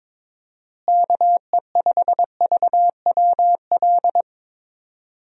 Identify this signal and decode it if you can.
Morse code, heard as a keyed tone, about 22 wpm, 700 Hz. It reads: KE5VWL
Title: USA Call Signs - 6 Characters - Repeated 2x - 22wpm